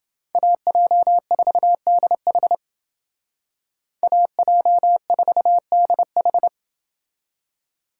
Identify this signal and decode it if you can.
Morse code, heard as a keyed tone, about 29 wpm, 700 Hz. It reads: AJ4DH AJ4DH